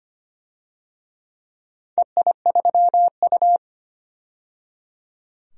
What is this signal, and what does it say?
Morse code, heard as a keyed tone, about 25 wpm, 700 Hz. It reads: EI3U
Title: DXCC-100 Call Signs - 4 Characters - 25wpm